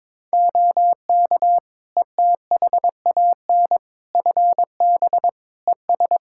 Morse code, heard as a keyed tone, about 22 wpm, 700 Hz. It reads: OK ETHAN FB ES
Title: Practice QSO - Exchange 2 - Info - 22wpm